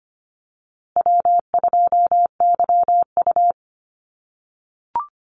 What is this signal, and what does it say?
Morse code, heard as a keyed tone, about 25 wpm, 700 Hz. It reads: W2YU
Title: USA Call Signs Repeated 3x 25wpm